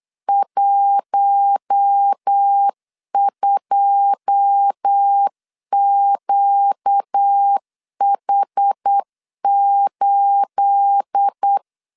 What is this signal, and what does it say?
Morse code, heard as a keyed tone, about 8 wpm, 800 Hz. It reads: 12QH8
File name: LEZIONI / 9 28 8wpm 3512